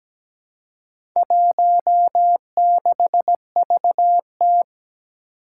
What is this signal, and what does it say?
Morse code, heard as a keyed tone, about 17 wpm, 700 Hz. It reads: 16VT
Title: Four Challenging Letter-Number Combinations 17wpm